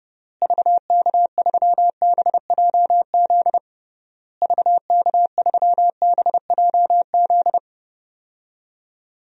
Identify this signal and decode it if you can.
Morse code, heard as a keyed tone, about 30 wpm, 700 Hz. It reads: VK3BJZ VK3BJZ
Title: DXCC-100 Call Signs - 6 Characters - Repeated 3x - 30wpm